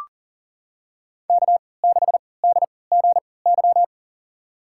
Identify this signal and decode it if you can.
Morse code, heard as a keyed tone, about 40 wpm, spaced as for 23 wpm, 700 Hz. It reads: K6DGY
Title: USA Call Signs - 5 Characters - 3x Character Spacing - 40wpm